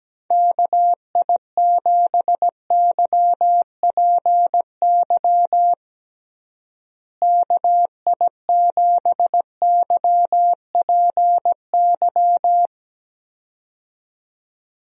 Morse code, heard as a keyed tone, about 17 wpm, 700 Hz. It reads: KI7YPY KI7YPY
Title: USA Call Signs - 6 Characters - Repeated 2x - 17wpm